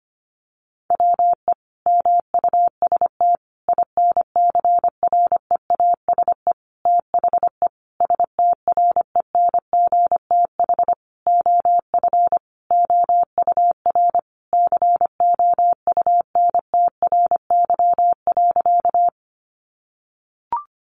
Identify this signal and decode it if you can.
Morse code, heard as a keyed tone, about 25 wpm, 700 Hz. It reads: WE MUST INCREASE THE STRENGTH OF OUR COUNTRY.
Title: Sentences from Top 1300 Words 25wpm